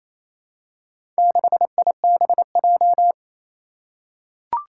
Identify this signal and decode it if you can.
Morse code, heard as a keyed tone, about 28 wpm, 700 Hz. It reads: 6IBJ
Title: Four Challenging Letter-Number Combinations 28wpm